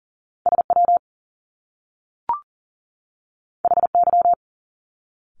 Morse code, heard as a keyed tone, about 40 wpm, 700 Hz. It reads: SW HY
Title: Top Two Letter Combinations in Words 40wpm